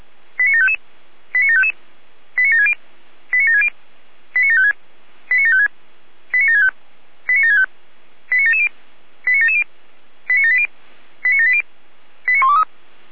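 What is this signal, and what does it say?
Five-tone selective call (ZVEI), heard as quick runs of five tones, tones of about 80 ms. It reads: 89755 89755 89760 89760 89756 89756 89765 89765 89770 89770 89780 89780 89123
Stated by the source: Five-tone ZVEI selective call 89725, received on SDR.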